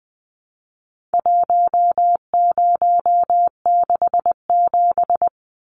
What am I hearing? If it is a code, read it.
Morse code, heard as a keyed tone, about 20 wpm, 700 Hz. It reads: 1067
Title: Numbers - 4-Digit 20wpm